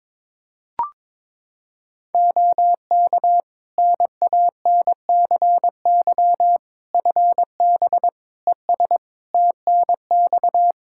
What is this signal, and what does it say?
Morse code, heard as a keyed tone, about 22 wpm, 700 Hz. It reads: OK NANCY FB ES TNX
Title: Practice QSO - Exchange 2 - Info - 22wpm